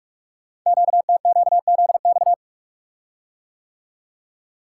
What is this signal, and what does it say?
Morse code, heard as a keyed tone, about 44 wpm, 700 Hz. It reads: QTQZX
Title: Code Groups - 5 Letters - 45wpm